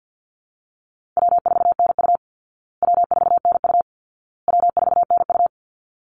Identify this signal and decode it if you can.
Morse code, heard as a keyed tone, about 49 wpm, 700 Hz. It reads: W4NU W4NU W4NU
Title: Easy Call Signs Repeated 3x - 50wpm